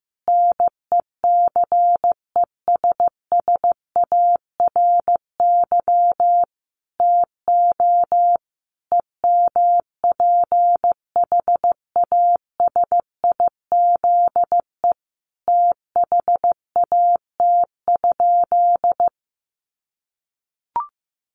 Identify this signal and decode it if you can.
Morse code, heard as a keyed tone, about 14 wpm, 700 Hz. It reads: NECESSARY TO EMPHASIZE THAT?
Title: Sentences from Top 1500 Words 15wpm